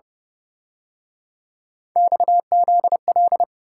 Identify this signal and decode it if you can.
Morse code, heard as a keyed tone, about 29 wpm, 700 Hz. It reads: XZL
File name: Speed-Racing - Three Letter-Number Combinations - 25wpm